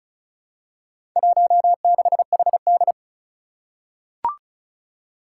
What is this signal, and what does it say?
Morse code, heard as a keyed tone, about 36 wpm, 700 Hz. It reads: 16HD
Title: Four Challenging Letter-Number Combinations 35wpm